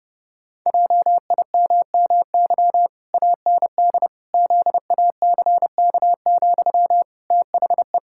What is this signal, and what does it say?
Morse code, heard as a keyed tone, about 29 wpm, 700 Hz. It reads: JIMMY AND ZACK, THE
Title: Pangram Challenge v2 - 30wpm